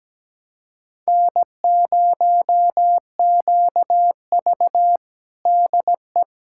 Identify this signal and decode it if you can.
Morse code, heard as a keyed tone, about 17 wpm, 700 Hz. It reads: N0QV DE
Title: USA Callsign DE Callsign 17wpm